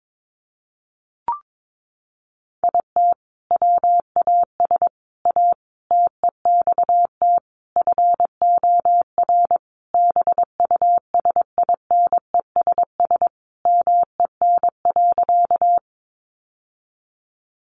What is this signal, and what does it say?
Morse code, heard as a keyed tone, about 22 wpm, 700 Hz. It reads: IT WAS A TEXT FOR BUSINESS MEN.
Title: Sentences from Top 1200 Words 22wpm